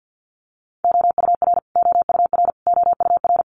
Morse code, heard as a keyed tone, about 50 wpm, 700 Hz. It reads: OUR OUR OUR
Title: Top 200 Words Repeated 50wpm